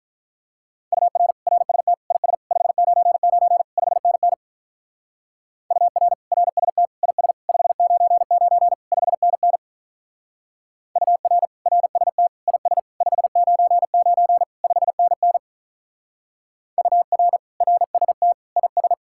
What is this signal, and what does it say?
Morse code, heard as a keyed tone, about 43 wpm, 700 Hz. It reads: UR RST IS 599 5NN UR RST IS 599 5NN UR RST IS 599 5NN UR RST IS
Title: Speed-Racing Common QSO Phrases - 35wpm